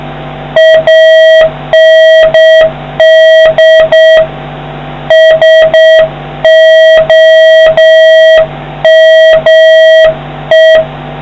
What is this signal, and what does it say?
Morse code, heard as a keyed tone, about 7 wpm, 1900 Hz, with noise. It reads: AND SOME